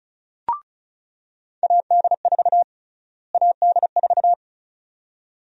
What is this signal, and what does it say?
Morse code, heard as a keyed tone, about 34 wpm, 700 Hz. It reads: AD4 AD4
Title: USA Call Sign Prefixes - Repeated 2x - 35wpm